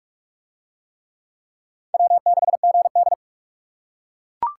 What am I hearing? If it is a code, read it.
Morse code, heard as a keyed tone, about 42 wpm, 700 Hz. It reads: W6GD